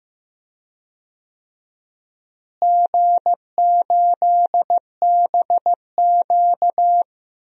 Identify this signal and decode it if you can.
Morse code, heard as a keyed tone, about 15 wpm, 700 Hz. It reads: G8BQ
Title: DXCC-100 Call Signs - 4 Characters - Repeated 2x - 15wpm